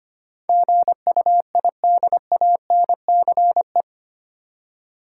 Morse code, heard as a keyed tone, about 25 wpm, 700 Hz. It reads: GUIDANCE
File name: Eight Letter Words - 25wpm